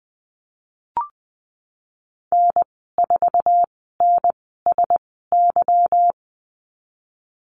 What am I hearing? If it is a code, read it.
Morse code, heard as a keyed tone, about 20 wpm, spaced as for 15 wpm, 700 Hz. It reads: N4NSY
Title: 173 Lesson 28g - Number 9 - Call Signs 20wpm